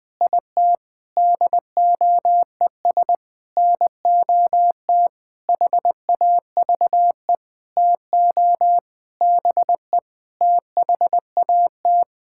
Morse code, heard as a keyed tone, about 20 wpm, 700 Hz. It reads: IT DOES NOT HAVE TO BE THAT
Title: Sentences with 8-Word Phrases - Spoken Beginning - 20wpm